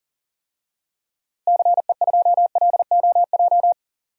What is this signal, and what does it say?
Morse code, heard as a keyed tone, about 40 wpm, 700 Hz. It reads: CE2LOJ
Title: DXCC-100 Call Signs - 6 Characters - Repeated 3x - 40wpm